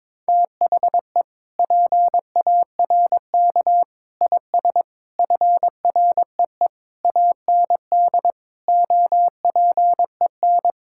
Morse code, heard as a keyed tone, about 22 wpm, 700 Hz. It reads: THE PARK IS FREE AND OPEN
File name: Sentences with 7-Word Phrases - 22wpm